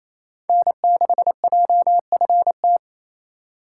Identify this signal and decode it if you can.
Morse code, heard as a keyed tone, about 28 wpm, 700 Hz. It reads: N6JFT
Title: USA Call Signs - 5 Characters - 2x Character Spacing - 28wpm